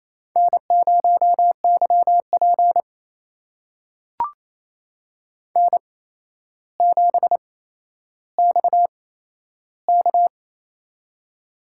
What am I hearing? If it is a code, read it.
Morse code, heard as a keyed tone, about 28 wpm, spaced as for 12 wpm, 700 Hz. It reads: N0YP N7XK
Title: USA Call Signs - 4 Characters - 8x Character Spacing - 28wpm